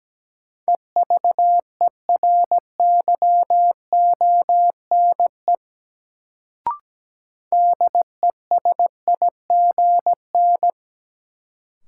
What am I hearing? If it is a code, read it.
Morse code, heard as a keyed tone, about 17 wpm, 700 Hz. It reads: EVERYONE DESIGN
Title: Sentences from Top 800 Words - New Word Drill - 17wpm